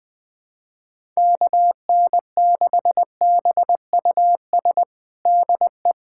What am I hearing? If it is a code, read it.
Morse code, heard as a keyed tone, about 20 wpm, 700 Hz. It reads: KN6BUS DE
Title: USA Callsign DE Callsign - 6 Characters - 20wpm